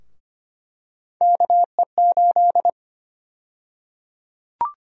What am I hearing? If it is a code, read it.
Morse code, heard as a keyed tone, about 25 wpm, 700 Hz. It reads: KE8
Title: USA Call Sign Prefixes - 25wpm